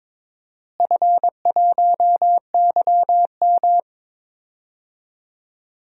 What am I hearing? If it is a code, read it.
Morse code, heard as a keyed tone, about 22 wpm, 700 Hz. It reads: F1YM
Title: DXCC-100 Call Signs - 4 Characters - Repeated 3x - 22wpm